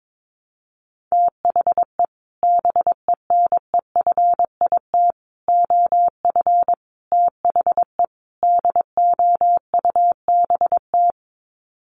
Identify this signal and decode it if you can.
Morse code, heard as a keyed tone, about 22 wpm, 700 Hz. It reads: THE BENEFIT OF THE DOUBT
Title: Sets of 5 Words Encore 22wpm